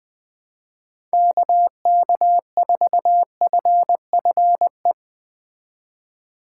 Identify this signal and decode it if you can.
Morse code, heard as a keyed tone, about 22 wpm, 700 Hz. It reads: KK4FFE
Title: Speed Racing - USA 2x3 Call Signs 20wpm